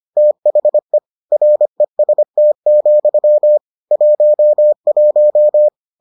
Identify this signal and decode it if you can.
Morse code, heard as a keyed tone, about 25 wpm, 600 Hz. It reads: THE REST, 11